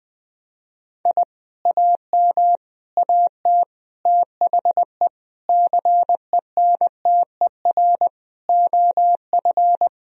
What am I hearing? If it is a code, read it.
Morse code, heard as a keyed tone, about 20 wpm, 700 Hz. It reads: I AM AT THE CENTER OF
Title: Sentences from Top 1000 Words - Review Part 2 - 20wpm